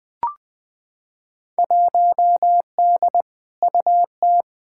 Morse code, heard as a keyed tone, about 20 wpm, 700 Hz. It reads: 1D UT
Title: Field Day Exchanges - 20wpm